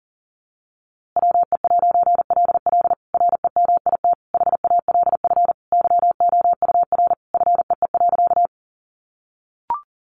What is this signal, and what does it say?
Morse code, heard as a keyed tone, about 40 wpm, 700 Hz. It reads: WE'LL REMIT HALF YOUR FEE.
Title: A Sound Of Thunder 40wpm